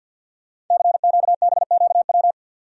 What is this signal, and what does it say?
Morse code, heard as a keyed tone, about 50 wpm, 700 Hz. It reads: CQBQW
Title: Code Groups - 5 Letter:Numbers - 50wpm